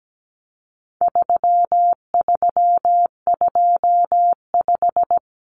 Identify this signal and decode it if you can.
Morse code, heard as a keyed tone, about 17 wpm, 700 Hz. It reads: 3325
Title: Common Cut Numbers - 4-Digit 17wpm